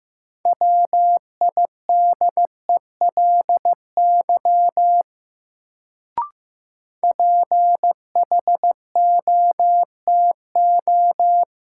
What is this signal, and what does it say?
Morse code, heard as a keyed tone, about 15 wpm, 700 Hz. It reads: WIDELY PHOTO
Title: Sentences from Top 1200 Words - New Word Drill - 15wpm